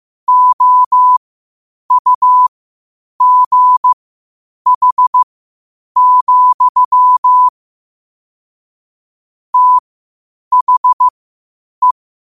Morse code, heard as a keyed tone, about 15 wpm, spaced as for 8 wpm, 1000 Hz. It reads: OUGH, THE
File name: qotdfiles / 8wpm